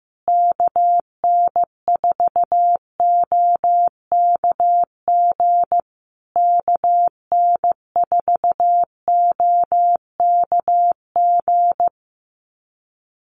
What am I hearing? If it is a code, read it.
Morse code, heard as a keyed tone, about 15 wpm, 700 Hz. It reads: KN4OKG KN4OKG
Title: USA Call Signs Repeated 3x 15wpm